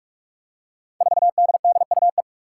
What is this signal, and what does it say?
Morse code, heard as a keyed tone, about 46 wpm, 700 Hz. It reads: 4DDUE
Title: Five Challenging Letter-Number Combinations 45wpm